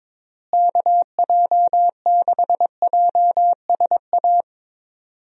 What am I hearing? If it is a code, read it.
Morse code, heard as a keyed tone, about 22 wpm, 700 Hz. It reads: KJ6JSA
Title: USA Call Signs - 6 Characters - Repeated 3x - 22wpm